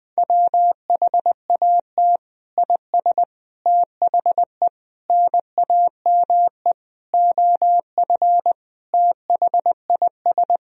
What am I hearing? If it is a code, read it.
Morse code, heard as a keyed tone, about 21 wpm, 700 Hz. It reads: WHAT IS THE NAME OF THIS
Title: Sentences from Top 1000 Words - Review Part 1 - 20wpm